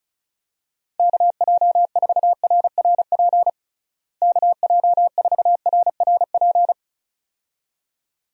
Morse code, heard as a keyed tone, about 34 wpm, 700 Hz. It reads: KJ4RRP KJ4RRP
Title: USA Call Signs - 6 Characters - Repeated 2x - 35wpm